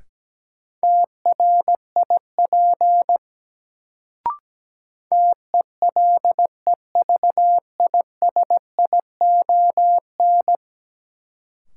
Morse code, heard as a keyed tone, about 17 wpm, 700 Hz. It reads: TRIP TELEVISION
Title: Sentences from Top 1000 Words - Review Part 4 - New Word Drill - 17wpm